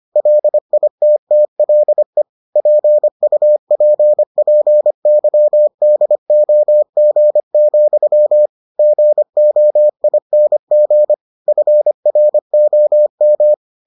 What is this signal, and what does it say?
Morse code, heard as a keyed tone, about 25 wpm, 600 Hz. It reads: LITTLE PUPPYDOG, GOING FROM